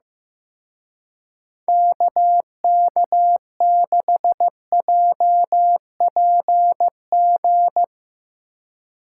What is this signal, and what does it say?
Morse code, heard as a keyed tone, about 15 wpm, 700 Hz. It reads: KK6JPG
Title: Speed Racing - USA Call Signs 15wpm